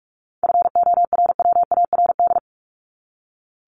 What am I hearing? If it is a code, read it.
Morse code, heard as a keyed tone, about 44 wpm, 700 Hz. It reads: FORWARD
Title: Top 1251-1500 Words 45wpm